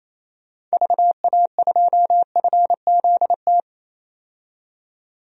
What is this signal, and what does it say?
Morse code, heard as a keyed tone, about 27 wpm, 700 Hz. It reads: VA2FZT
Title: DXCC-100 Call Signs - 6 Characters - 28wpm